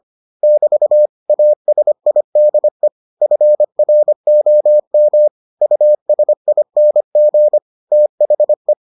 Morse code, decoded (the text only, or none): <BT> ASIDE FROM USING THE